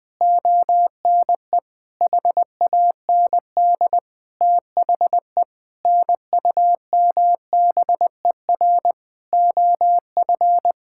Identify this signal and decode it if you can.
Morse code, heard as a keyed tone, about 21 wpm, 700 Hz. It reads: ONE HAND THE NUMBER OF
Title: Sentences with 9-Word Phrases - Spoken Beginning - 20wpm